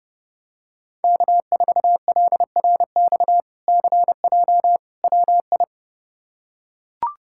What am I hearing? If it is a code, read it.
Morse code, heard as a keyed tone, about 30 wpm, 700 Hz. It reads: K4LRX CJ WI
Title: K1USN SST QSO Practice - 30wpm